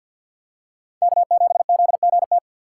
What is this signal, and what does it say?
Morse code, heard as a keyed tone, about 47 wpm, 700 Hz. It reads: K7ZGT
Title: USA Call Signs - 5 Characters - 4x Character Spacing - 50wpm